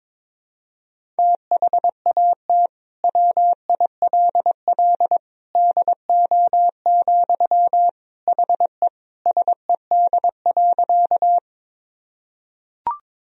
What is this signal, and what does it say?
Morse code, heard as a keyed tone, about 21 wpm, 700 Hz. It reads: THAT WILL DO, HE SED.
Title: Sentences from Top 100 Words and Abbreviations 22wpm